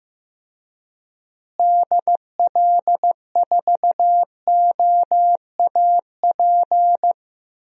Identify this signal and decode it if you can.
Morse code, heard as a keyed tone, about 15 wpm, 700 Hz. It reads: DL4OAP
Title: USA Call Signs - 6 Characters - Repeated 3x - 15wpm